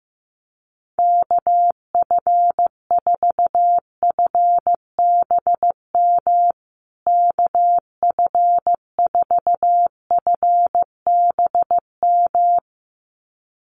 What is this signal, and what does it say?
Morse code, heard as a keyed tone, about 15 wpm, 700 Hz. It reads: KF4FBM KF4FBM
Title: USA Call Signs Repeated 2x 15wpm